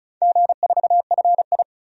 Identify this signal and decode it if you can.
Morse code, heard as a keyed tone, about 36 wpm, 700 Hz. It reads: G4FI
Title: DXCC-100 Call Signs - 4 Characters - 35wpm